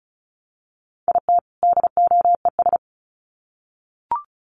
Morse code, heard as a keyed tone, about 34 wpm, 700 Hz. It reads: IT DOES
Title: Sets of 2 Words 35wpm